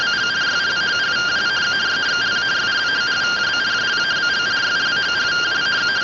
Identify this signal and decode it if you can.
Two-tone teleprinter signal, about 170 Hz shift.
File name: findings / CT 1 Scout 0001.gen-32 rtty rx